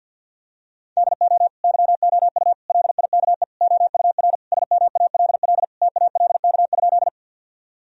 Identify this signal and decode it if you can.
Morse code, heard as a keyed tone, about 51 wpm, 700 Hz. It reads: DO YOU LIKE OUR SMALL TALK?
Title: Sentences from Top 1000 Words - Review Part 1 - 50wpm